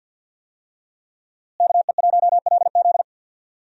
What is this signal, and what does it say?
Morse code, heard as a keyed tone, about 53 wpm, 700 Hz. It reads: KE1LZ
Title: USA Call Signs - 5 Characters - Repeated 2x - 50wpm